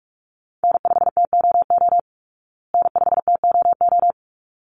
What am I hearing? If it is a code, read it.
Morse code, heard as a keyed tone, about 44 wpm, 700 Hz. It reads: N5TOO N5TOO
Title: Call Signs Repeated 3x - 45wpm